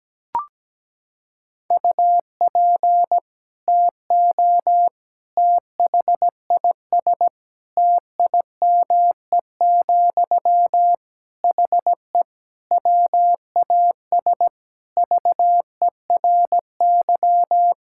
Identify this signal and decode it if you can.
Morse code, heard as a keyed tone, about 17 wpm, 700 Hz. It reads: UP TO THIS TIME, HE WAS VERY
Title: Sentences from Top 1000 Words - Review Part 6 - 17wpm